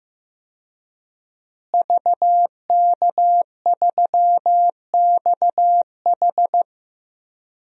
Morse code, heard as a keyed tone, about 15 wpm, 700 Hz. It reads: VK3XH
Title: DXCC-100 Call Signs - 5 Characters - Repeated 3x - 15wpm